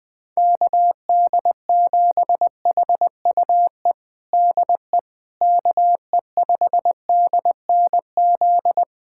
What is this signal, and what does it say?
Morse code, heard as a keyed tone, about 20 wpm, 700 Hz. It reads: KD7HUE DE KE5DNZ
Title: USA Callsign DE Callsign - 6 Characters - 20wpm